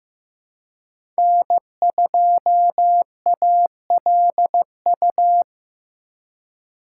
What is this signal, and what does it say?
Morse code, heard as a keyed tone, about 15 wpm, 700 Hz. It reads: N2ALU
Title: USA Call Signs - 3x Character Spacing - 15wpm